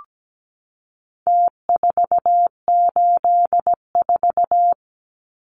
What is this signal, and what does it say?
Morse code, heard as a keyed tone, about 17 wpm, 700 Hz. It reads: T484
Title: Numbers - Contest Style - 4-Digit 17wpm